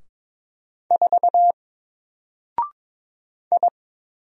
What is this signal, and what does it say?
Morse code, heard as a keyed tone, about 22 wpm, 700 Hz. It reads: 4 I
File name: Single-Letter-Number-22wpm